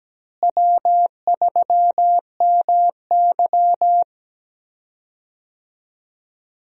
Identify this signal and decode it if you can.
Morse code, heard as a keyed tone, about 17 wpm, 700 Hz. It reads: W3MY
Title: USA Call Signs - 4 Characters - Repeated 3x - 17wpm